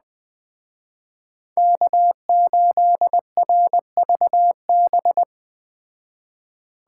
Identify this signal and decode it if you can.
Morse code, heard as a keyed tone, about 21 wpm, 700 Hz. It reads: K8RVB